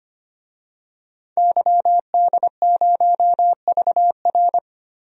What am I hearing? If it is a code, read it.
Morse code, heard as a keyed tone, about 25 wpm, 700 Hz. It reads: YD0VR
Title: DXCC-100 Call Signs - 5 Characters - 25wpm